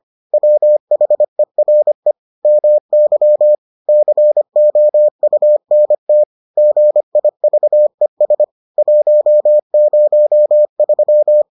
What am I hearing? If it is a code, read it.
Morse code, heard as a keyed tone, about 25 wpm, 600 Hz. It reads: WHERE MY COUNT GIVES 103